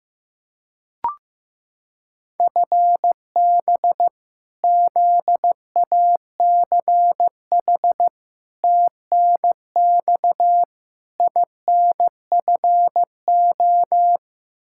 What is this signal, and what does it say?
Morse code, heard as a keyed tone, about 15 wpm, 700 Hz. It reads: FB ZACH TNX INFO
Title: K1USN SST QSO Practice 15wpm